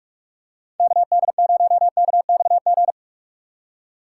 Morse code, heard as a keyed tone, about 44 wpm, 700 Hz. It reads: KD0KXG